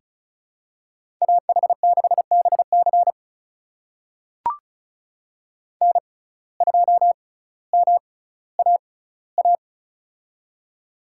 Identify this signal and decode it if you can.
Morse code, heard as a keyed tone, about 36 wpm, spaced as for 18 wpm, 700 Hz. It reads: AH6BC N2MAA
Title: USA Call Signs - 5 Characters - 6x Character Spacing - 35wpm